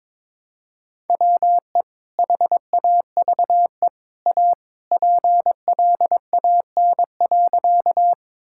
Morse code, heard as a keyed tone, about 22 wpm, 700 Hz. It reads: WE HAVE A PLAN.
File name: practice / 22wpm / Sentences with 2-Word Phrases - 22wpm